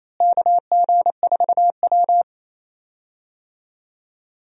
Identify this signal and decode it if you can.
Morse code, heard as a keyed tone, about 28 wpm, 700 Hz. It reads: KG4W
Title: USA Call Signs - 4 Characters - Repeated 3x - 28wpm